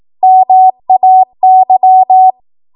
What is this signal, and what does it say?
Morse code, heard as a keyed tone, about 18 wpm, 750 Hz. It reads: MAY
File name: Morse / 18WPM